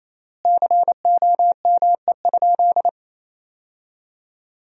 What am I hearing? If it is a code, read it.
Morse code, heard as a keyed tone, about 28 wpm, 700 Hz. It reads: COME?
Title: 3x Word-Spacing - Sentences from Top 200 Words - 28wpm